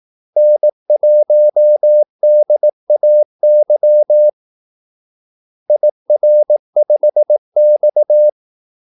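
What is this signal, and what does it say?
Morse code, heard as a keyed tone, about 18 wpm, 600 Hz. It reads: N1DAY IR5X
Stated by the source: Morse call-sign practice at 18 wpm.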